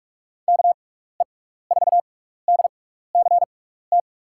Morse code, heard as a keyed tone, about 46 wpm, spaced as for 16 wpm, 700 Hz. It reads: KE4DCT
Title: USA Call Signs - 6 Characters - 6x Character Spacing - 45wpm